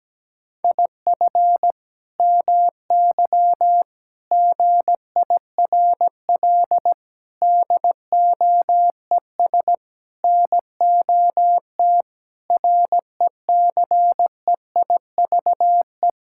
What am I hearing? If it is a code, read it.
Morse code, heard as a keyed tone, about 17 wpm, 700 Hz. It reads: IF MY GIRL DOES NOT RECEIVE
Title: Sentences from Top 1600 Words 17wpm